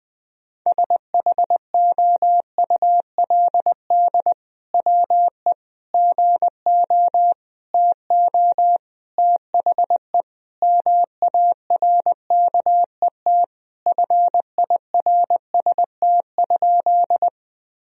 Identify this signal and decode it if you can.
Morse code, heard as a keyed tone, about 21 wpm, 700 Hz. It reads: SHOULD WE GO TO THE MARKET FIRST?
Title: Sentences from Top 1000 Words - Review Part 3 - 20wpm